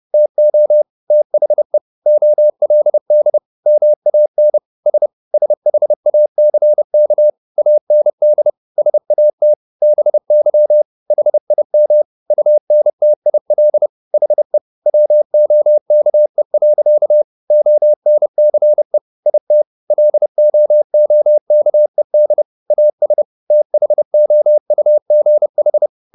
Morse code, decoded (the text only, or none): TO THE OLD MAN S SHACK AND SAT BY HIM UNTIL HE WOKE. ONCE IT LOOKED AS THOUGH